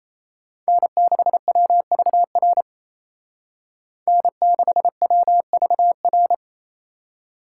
Morse code, heard as a keyed tone, about 35 wpm, 700 Hz. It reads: N6WVR N6WVR